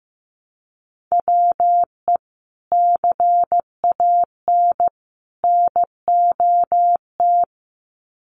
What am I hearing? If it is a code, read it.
Morse code, heard as a keyed tone, about 15 wpm, 700 Hz. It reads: WE CAN NOT
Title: Sets of 3 Words 15wpm